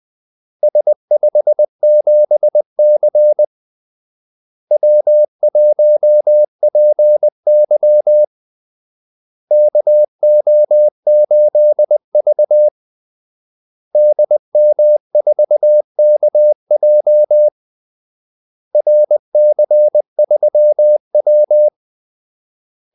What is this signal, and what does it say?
Morse code, heard as a keyed tone, about 20 wpm, 600 Hz. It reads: S57C W1PY KO8V DM4KJ RC3W
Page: Kaldesignaler 20 wpm